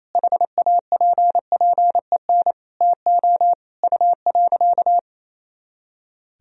Morse code, heard as a keyed tone, about 28 wpm, 700 Hz. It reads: HAPPEN TO U.
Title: Sentences from Top 500 Words and Abbreviations 28wpm